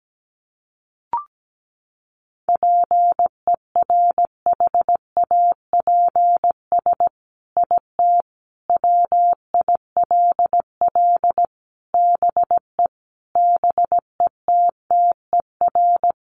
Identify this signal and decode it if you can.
Morse code, heard as a keyed tone, about 17 wpm, 700 Hz. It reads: PERHAPS IT WILL BE BETTER
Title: Sentences from Top 1500 Words 17wpm